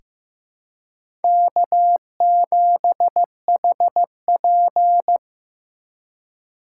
Morse code, heard as a keyed tone, about 15 wpm, 700 Hz. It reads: K7HP
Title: USA Call Signs - 4 Characters - 4x Character Spacing - 15wpm